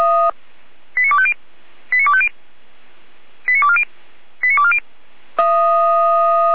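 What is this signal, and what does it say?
Five-tone selective call (ZVEI), heard as quick runs of five tones, tones of about 70 ms. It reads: 89250 89250 89240 89240
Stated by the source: ZVEI five-tone selective call 89276 received on SDR.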